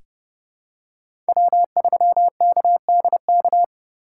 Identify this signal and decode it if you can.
Morse code, heard as a keyed tone, about 30 wpm, 700 Hz. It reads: W3KDK
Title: USA Call Signs - 5 Characters - 2x Character Spacing - 30wpm